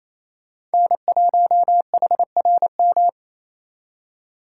Morse code, heard as a keyed tone, about 36 wpm, 700 Hz. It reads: N1HRM